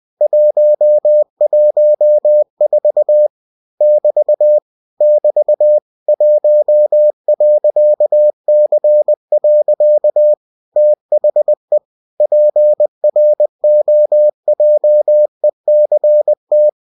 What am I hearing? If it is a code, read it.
Morse code, heard as a keyed tone, about 20 wpm, 600 Hz. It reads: 114 <BT> <BT> 1.C. THE PROJECT
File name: cw-books / winnie-the-pooh / winnie-20wpm-page-0114